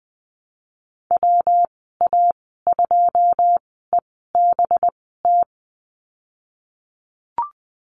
Morse code, heard as a keyed tone, about 20 wpm, spaced as for 15 wpm, 700 Hz. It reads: WA2EBT
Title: 186 Lesson 30g - Letter b - Call Signs 20wpm